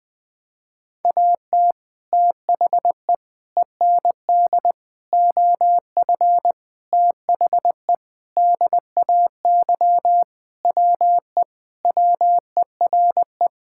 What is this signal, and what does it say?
Morse code, heard as a keyed tone, about 20 wpm, 700 Hz. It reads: AT THE END OF THE DAY WE WERE
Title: Sentences with 5-Word Phrases - 20wpm